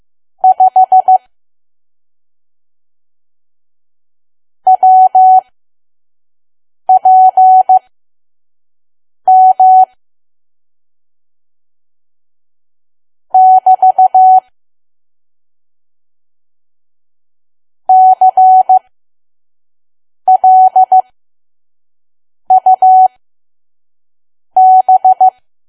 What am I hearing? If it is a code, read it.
Morse code, heard as a keyed tone, about 15 wpm, spaced as for 5 wpm, 750 Hz. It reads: 5 WPM <BT> CLUB
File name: Archive / 5 WPM files / PM